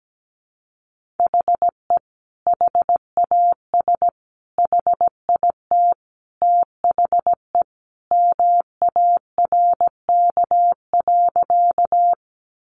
Morse code, heard as a keyed tone, about 17 wpm, 700 Hz. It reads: HE HAS HIT THE MARK.
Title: Sentences from Top 1200 Words 17wpm